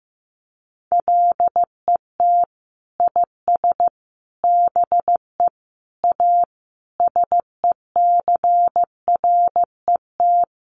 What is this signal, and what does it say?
Morse code, heard as a keyed tone, about 14 wpm, 700 Hz. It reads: LET IS BE A SECRET
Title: Sentences from Top 1200 Words 15wpm